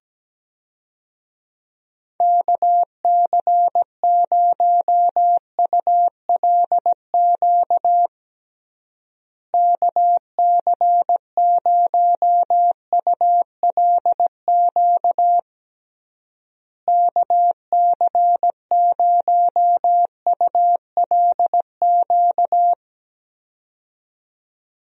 Morse code, heard as a keyed tone, about 17 wpm, 700 Hz. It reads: KC0ULQ KC0ULQ KC0ULQ
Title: USA Call Signs - 6 Characters - Repeated 3x - 17wpm